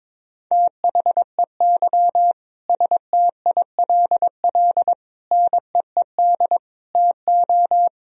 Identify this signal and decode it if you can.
Morse code, heard as a keyed tone, about 22 wpm, 700 Hz. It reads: THEY STILL NEED TO
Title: Sentences from Top 1000 Words - Review Part 4 - 22wpm